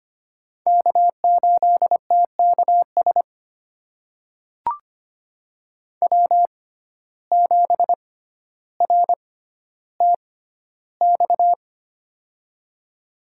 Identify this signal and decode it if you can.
Morse code, heard as a keyed tone, about 25 wpm, spaced as for 13 wpm, 700 Hz. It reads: K8TKS W7RTX
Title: USA Call Signs - 6x Character Spacing - 25wpm